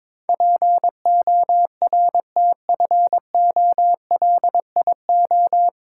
Morse code, heard as a keyed tone, about 22 wpm, 700 Hz. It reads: PORTFOLIO
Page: Morse Code Ninja practice 22wpm